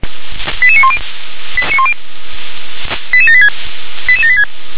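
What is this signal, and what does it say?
Five-tone selective call (ZVEI), heard as quick runs of five tones, tones of about 70 ms. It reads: 88910 88786 88786
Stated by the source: ZVEI five-tone selective call received by SDR.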